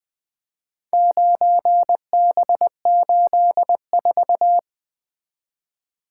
Morse code, heard as a keyed tone, about 22 wpm, 700 Hz. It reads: 9B84
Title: Speed-Racing - Four Letter-Number Combinations - 20wpm